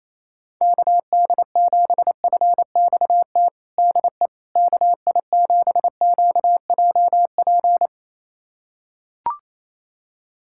Morse code, heard as a keyed tone, about 28 wpm, 700 Hz. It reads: KD7FXT DE KI7QJP
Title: USA Callsign DE Callsign - 6 Characters - 28wpm